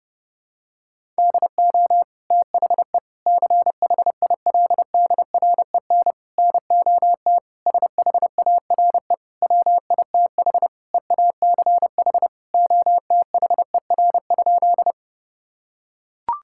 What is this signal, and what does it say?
Morse code, heard as a keyed tone, about 30 wpm, 700 Hz. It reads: DO THE CHILDREN NOT SHARE WITH EACH OTHER?
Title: Sentences from Top 1000 Words - Review Part 6 - 30wpm